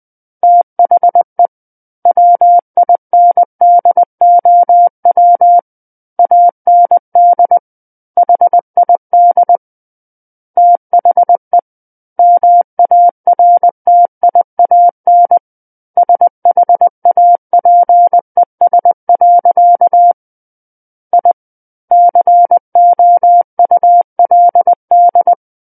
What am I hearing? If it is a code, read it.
Morse code, heard as a keyed tone, about 20 wpm, 700 Hz. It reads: THE WINDOW AND HID THE MARTIAN SHAPES. I COULD
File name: 20 WPM